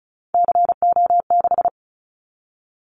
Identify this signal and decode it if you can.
Morse code, heard as a keyed tone, about 34 wpm, 700 Hz. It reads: CO6
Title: Callsign Prefixes for Top 100 DXCC - 35wpm